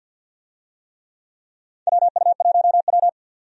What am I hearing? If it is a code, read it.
Morse code, heard as a keyed tone, about 50 wpm, 700 Hz. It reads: WU1W
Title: USA Call Signs - 4 Characters - 50wpm